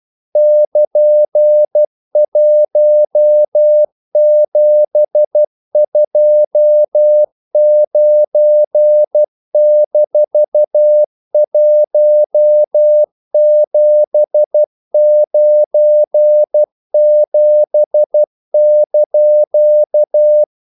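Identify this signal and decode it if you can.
Morse code, heard as a keyed tone, about 12 wpm, 600 Hz. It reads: <KN>1729-1797)
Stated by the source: Morse practice sending English quotations.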